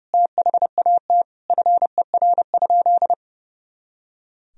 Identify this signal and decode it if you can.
Morse code, heard as a keyed tone, about 30 wpm, 700 Hz. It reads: THAT FER?
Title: Sentences from Top 200 Words and Abbreviations 30wpm